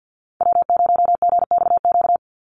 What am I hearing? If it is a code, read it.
Morse code, heard as a keyed tone, about 49 wpm, 700 Hz. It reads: W0GXQ
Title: Call Signs Repeated 3x - 50wpm